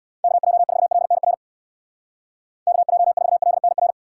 Morse code, heard as a keyed tone, about 65 wpm, 700 Hz. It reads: KJ4LNR KJ4LNR